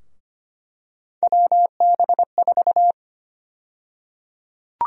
Morse code, heard as a keyed tone, about 25 wpm, 700 Hz. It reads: WB4
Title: USA Call Sign Prefixes - Repeated 2x - 25wpm